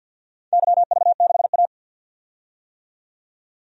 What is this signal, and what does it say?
Morse code, heard as a keyed tone, about 52 wpm, 700 Hz. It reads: YV6A
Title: DXCC-100 Call Signs - 4 Characters - 50wpm